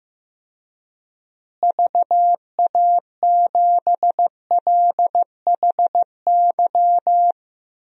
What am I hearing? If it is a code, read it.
Morse code, heard as a keyed tone, about 15 wpm, 700 Hz. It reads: VA7LHY
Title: DXCC-100 Call Signs - 6 Characters - Repeated 3x - 15wpm